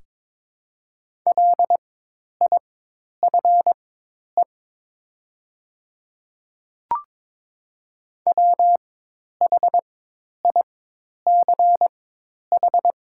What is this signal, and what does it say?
Morse code, heard as a keyed tone, about 22 wpm, spaced as for 10 wpm, 700 Hz. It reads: LIFE WHICH
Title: 4x Character Spacing - Top 300 Words - 22wpm